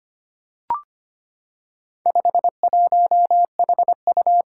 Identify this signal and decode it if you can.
Morse code, heard as a keyed tone, about 25 wpm, 700 Hz. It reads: 51HU
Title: Four Challenging Letter-Number Combinations 25wpm